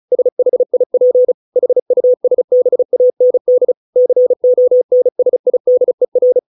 Morse code, heard as a keyed tone, about 35 wpm, 500 Hz. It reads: SHIP HUSBAND CONSIDER